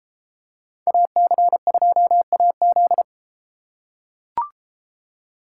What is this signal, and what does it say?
Morse code, heard as a keyed tone, about 33 wpm, 700 Hz. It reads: AC2AZ